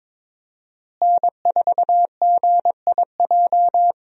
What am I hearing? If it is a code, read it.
Morse code, heard as a keyed tone, about 22 wpm, 700 Hz. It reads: N4GIJ
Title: USA Call Signs - 5 Characters - Repeated 2x - 22wpm